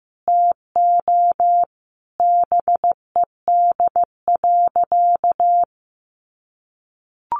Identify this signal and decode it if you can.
Morse code, heard as a keyed tone, about 15 wpm, 700 Hz. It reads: TO BED.
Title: Sentences from Top 700 Words 15wpm